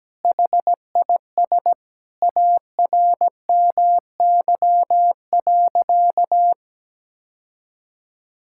Morse code, heard as a keyed tone, about 17 wpm, 700 Hz. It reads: HIS ARMY.
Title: Sentences from Top 1000 Words - Review Part 6 - 17wpm